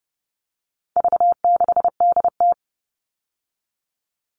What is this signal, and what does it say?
Morse code, heard as a keyed tone, about 30 wpm, 700 Hz. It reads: V6DT
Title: All Cut Numbers - 4-Digit 30wpm